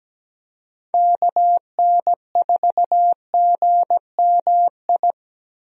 Morse code, heard as a keyed tone, about 17 wpm, 700 Hz. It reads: KN4GMI
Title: USA Call Signs - 6 Characters - Repeated 3x - 17wpm